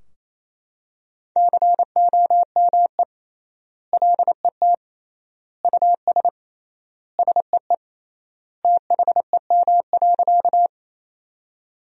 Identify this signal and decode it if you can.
Morse code, heard as a keyed tone, about 28 wpm, 700 Hz. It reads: COME LET US SEE THEM.
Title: 3x Word-Spacing - Sentences from Top 300 Words - 28wpm